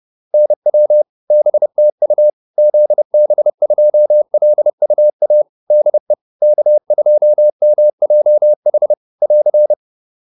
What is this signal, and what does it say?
Morse code, heard as a keyed tone, about 30 wpm, 600 Hz. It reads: NW BTU ZB2LUA DE K2MJH <AR>
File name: QSO402 30